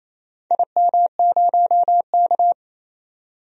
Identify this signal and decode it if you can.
Morse code, heard as a keyed tone, about 28 wpm, 700 Hz. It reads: IM0K
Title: DXCC-100 Call Signs - 4 Characters - 28wpm